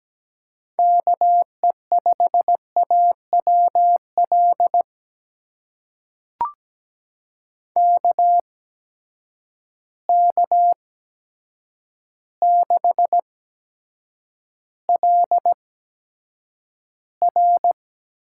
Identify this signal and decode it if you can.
Morse code, heard as a keyed tone, about 17 wpm, spaced as for 8 wpm, 700 Hz. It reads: KE5AWL KK6LR
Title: USA Call Signs - 8x Character Spacing - 17wpm